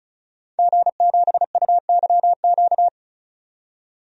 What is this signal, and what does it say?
Morse code, heard as a keyed tone, about 34 wpm, 700 Hz. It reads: G7UYQ
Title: DXCC-100 Call Signs - 5 Characters - 35wpm